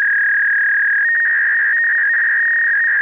Fax machine handshake.